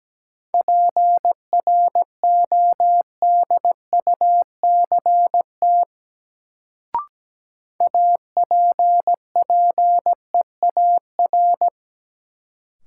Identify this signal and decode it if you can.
Morse code, heard as a keyed tone, about 17 wpm, 700 Hz. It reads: PRODUCT APPEAR
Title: Sentences from Top 1000 Words - Review Part 2 - New Word Drill - 17wpm